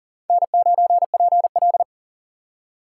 Morse code, heard as a keyed tone, about 43 wpm, 690 Hz. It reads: N9PL
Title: USA Call Signs - 4 Characters - 6x Character Spacing - 40wpm